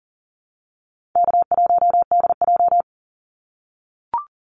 Morse code, heard as a keyed tone, about 40 wpm, 700 Hz. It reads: K1DJ
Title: Easy Call Signs - 40wpm